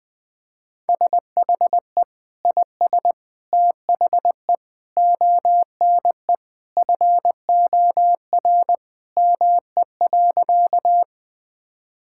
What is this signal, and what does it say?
Morse code, heard as a keyed tone, about 20 wpm, 700 Hz. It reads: SHE IS THE ONE FOR ME.
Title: Sentences with 2-Word Phrases - Spoken Beginning - 20wpm